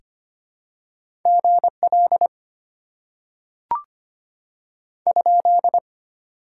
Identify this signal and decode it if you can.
Morse code, heard as a keyed tone, about 25 wpm, 700 Hz. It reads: GL ?
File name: Top 90 QSO Elements - 6x Character Spacing - 25wpm